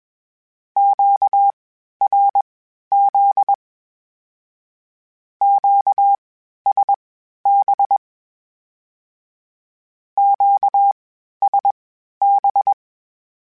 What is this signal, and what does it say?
Morse code, heard as a keyed tone, about 21 wpm, spaced as for 12 wpm, 800 Hz. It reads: QRZ QSB QSB
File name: (spaced) / Q-Calls Spaced-mp3s / Q Calls Twice sp @20